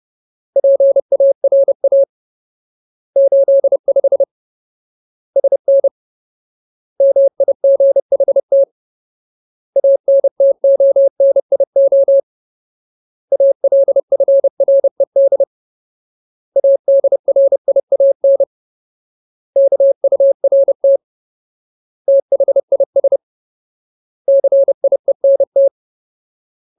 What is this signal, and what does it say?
Morse code, heard as a keyed tone, about 30 wpm, 550 Hz. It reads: PARA 85 SN MIGHT ANTONIO ALFRED ADRIAN KURT THIS CIENT
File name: Ultimate Word List / 30WPM Ultimate